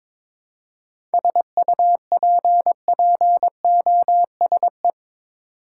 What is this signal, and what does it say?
Morse code, heard as a keyed tone, about 22 wpm, 700 Hz. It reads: SUPPOSE
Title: Seven Letter Words - 22wpm